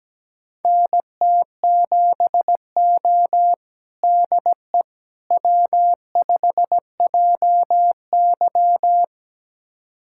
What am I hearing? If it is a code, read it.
Morse code, heard as a keyed tone, about 17 wpm, 700 Hz. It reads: NT7O DE W5JY
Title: USA Callsign DE Callsign - 4 Characters - 17wpm